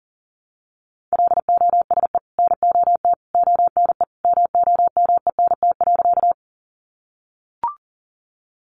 Morse code, heard as a keyed tone, about 40 wpm, 700 Hz. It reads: LOSE NOT ONE MOMENT.
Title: Sentences from Top 1100 Words 40wpm